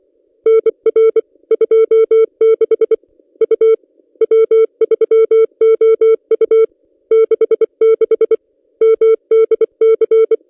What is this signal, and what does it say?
Morse code, heard as a keyed tone, about 24 wpm, 450 Hz. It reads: NR 26 U W3OU 66 MDC
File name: contests / ss-cw-k5tr-op-n5rz -1800